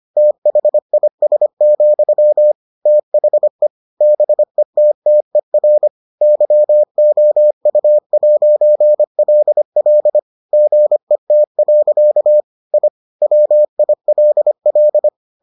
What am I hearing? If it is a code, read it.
Morse code, heard as a keyed tone, about 25 wpm, 600 Hz. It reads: THIS, THE BETTER YOU'LL GET. I WILL